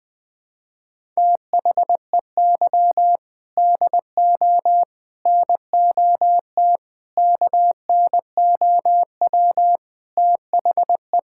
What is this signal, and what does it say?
Morse code, heard as a keyed tone, about 20 wpm, 700 Hz. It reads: THEY DO NOT KNOW THE
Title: Sentences with 4-Word Phrases - 20wpm